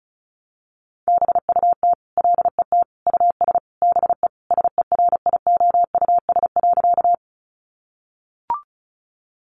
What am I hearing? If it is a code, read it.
Morse code, heard as a keyed tone, about 34 wpm, 700 Hz. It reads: BUT LET US BE SERIOUS.
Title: Sentences from Top 1100 Words 35wpm